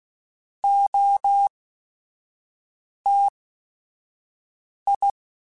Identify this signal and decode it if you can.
Morse code, heard as a keyed tone, about 16 wpm, spaced as for 5 wpm, 800 Hz. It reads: OTI
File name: Technical Subjects / Technical Subjects-mp3s / XL @5